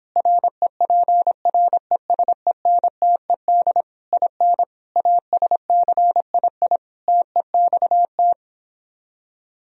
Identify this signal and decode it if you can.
Morse code, heard as a keyed tone, about 26 wpm, 700 Hz. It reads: REPRESENTED IN ASCII TEXT